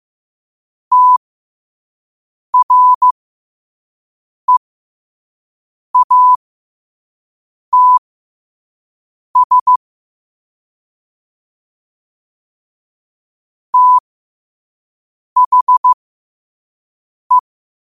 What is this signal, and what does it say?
Morse code, heard as a keyed tone, about 15 wpm, spaced as for 5 wpm, 1000 Hz. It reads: TREATS THE
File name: qotdfiles / 5wpm